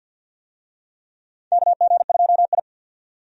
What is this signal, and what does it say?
Morse code, heard as a keyed tone, about 46 wpm, 700 Hz. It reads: KG2I
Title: USA Call Signs - 4 Characters - 50wpm